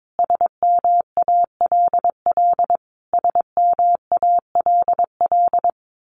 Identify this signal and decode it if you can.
Morse code, heard as a keyed tone, about 22 wpm, 700 Hz. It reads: SMALL SMALL
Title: Top 500 Words Repeated 22wpm